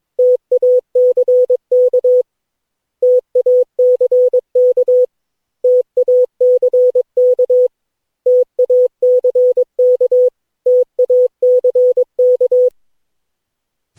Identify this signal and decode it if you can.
Morse code, heard as a keyed tone, about 21 wpm, 500 Hz. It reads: TACK TACK TACK TACK TACK
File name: cw / Session9 cw training